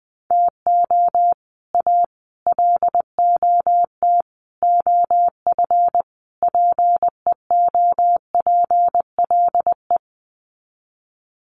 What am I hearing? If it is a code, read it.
Morse code, heard as a keyed tone, about 20 wpm, 700 Hz. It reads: TO A LOT OF PEOPLE
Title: Sets of 5 Words Encore 20wpm